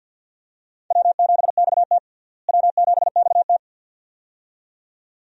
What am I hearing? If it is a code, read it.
Morse code, heard as a keyed tone, about 47 wpm, 700 Hz. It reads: W7XT W7XT
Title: USA Call Signs - 4 Characters - Repeated 2x - 50wpm